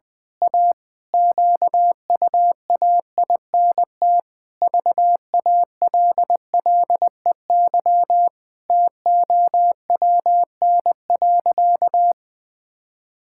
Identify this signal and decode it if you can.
Morse code, heard as a keyed tone, about 20 wpm, 700 Hz. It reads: A QUAINT VALLEY TOWN.
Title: Pangram Challenge v2 - 20wpm